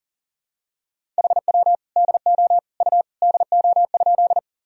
Morse code, heard as a keyed tone, about 40 wpm, 700 Hz. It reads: HW DO U DO?
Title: Sentences from Top 100 Words and Abbreviations 40wpm